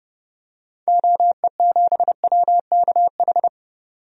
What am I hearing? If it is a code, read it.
Morse code, heard as a keyed tone, about 30 wpm, 700 Hz. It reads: OE7WKH
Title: DXCC-100 Call Signs - 6 Characters - 30wpm